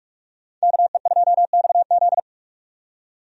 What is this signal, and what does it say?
Morse code, heard as a keyed tone, about 45 wpm, 700 Hz. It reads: KE2XZ
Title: USA Call Signs - 5 Characters - 45wpm